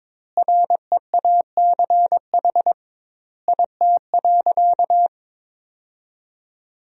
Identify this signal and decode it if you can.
Morse code, heard as a keyed tone, about 21 wpm, 700 Hz. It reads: REACH IT.
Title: 2x Word-Spacing - Sentences from Top 400 Words - 22wpm